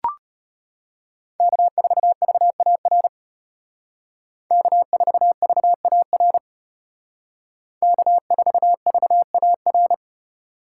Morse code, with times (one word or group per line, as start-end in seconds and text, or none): 1.4-3.1 K4VAR
4.5-6.4 K4VAR
7.8-9.9 K4VAR